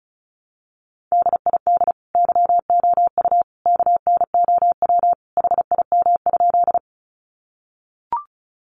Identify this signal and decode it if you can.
Morse code, heard as a keyed tone, about 34 wpm, 700 Hz. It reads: DID YOU KNOW HIM?
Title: Sentences from Top 100 Words 35wpm